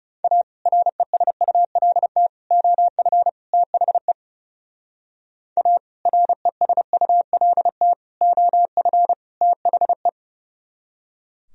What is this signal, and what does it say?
Morse code, heard as a keyed tone, about 32 wpm, 700 Hz. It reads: A RESULT OF THE A RESULT OF THE
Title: Speed-Racing - Sets of 4 Words 30wpm